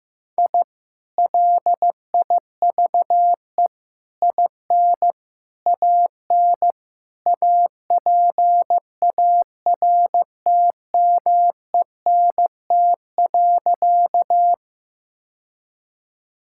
Morse code, heard as a keyed tone, about 15 wpm, 700 Hz. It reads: I LIVE IN AN APARTMENT.
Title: Sentences from Top 1000 Words - Review Part 6 - 15wpm